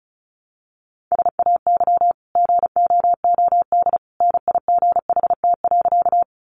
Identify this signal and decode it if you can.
Morse code, heard as a keyed tone, about 34 wpm, 700 Hz. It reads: SAY GOOD NIGHT.
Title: Sentences from Top 300 Words 35wpm